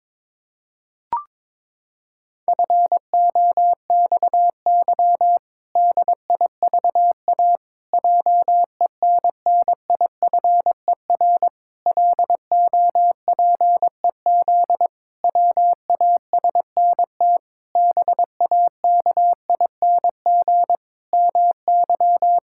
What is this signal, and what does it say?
Morse code, heard as a keyed tone, about 22 wpm, 700 Hz. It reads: FOXY DIVA JENNIFER LOPEZ WASNT BAKING MY
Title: Pangram Challenge v2 - 22wpm